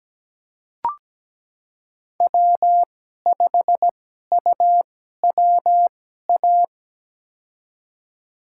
Morse code, heard as a keyed tone, about 17 wpm, spaced as for 12 wpm, 700 Hz. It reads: W5UWA
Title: USA Call Signs - 5 Characters - 2x Character Spacing - 17wpm